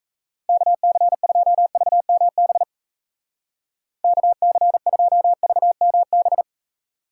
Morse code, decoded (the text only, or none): KC2VMB KC2VMB